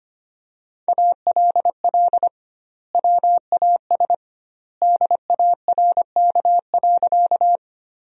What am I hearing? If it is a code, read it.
Morse code, heard as a keyed tone, about 25 wpm, 700 Hz. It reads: ALL WAS DARK.
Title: 2x Word-Spacing - Sentences from Top 500 Words - 25wpm